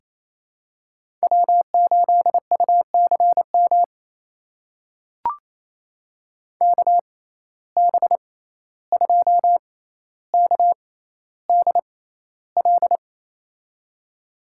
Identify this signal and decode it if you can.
Morse code, heard as a keyed tone, about 28 wpm, spaced as for 14 wpm, 700 Hz. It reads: W8UCM KB2KDL
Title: USA Call Signs - 6x Character Spacing - 28wpm